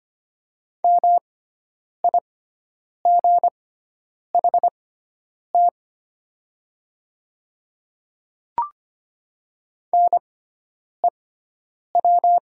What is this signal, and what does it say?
Morse code, heard as a keyed tone, about 25 wpm, spaced as for 9 wpm, 700 Hz. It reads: MIGHT NEW
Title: 6x Character Spacing - Top 300 Words - 25wpm